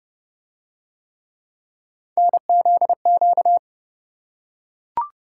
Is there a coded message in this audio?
Yes: Morse code from 2.2 to 3.6 s, heard as a keyed tone, about 29 wpm, 700 Hz. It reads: NZQ